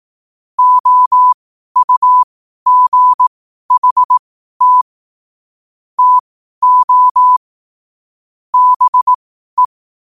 Morse code, heard as a keyed tone, about 18 wpm, spaced as for 12 wpm, 1000 Hz. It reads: OUGHT TO BE
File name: qotdfiles / 12wpm